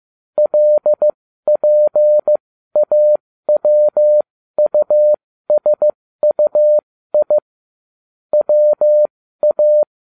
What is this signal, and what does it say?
Morse code, heard as a keyed tone, about 15 wpm, spaced as for 13 wpm, 600 Hz. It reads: LPAWUSUI WA
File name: cw / ubacw12 1